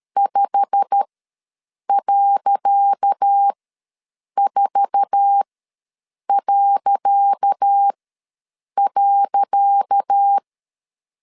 Morse code, heard as a keyed tone, about 13 wpm, spaced as for 7 wpm, 800 Hz. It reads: 5.4..
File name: LEZIONI / 10 SIGNS / 10C DOT ws 12wpm 3548